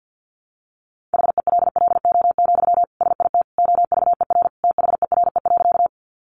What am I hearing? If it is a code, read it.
Morse code, heard as a keyed tone, about 50 wpm, 700 Hz. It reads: HELLO, SIT OVER THERE.
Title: Sentences from Top 1100 Words 50wpm